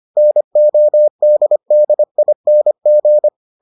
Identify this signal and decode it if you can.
Morse code, heard as a keyed tone, about 25 wpm, 600 Hz. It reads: NODDING